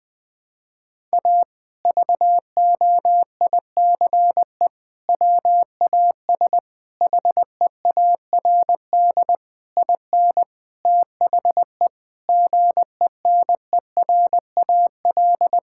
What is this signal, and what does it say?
Morse code, heard as a keyed tone, about 21 wpm, 700 Hz. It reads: A VOICE WAS HEARD IN THE GENERAL
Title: Sentences from Top 1000 Words - Review Part 5 - 20wpm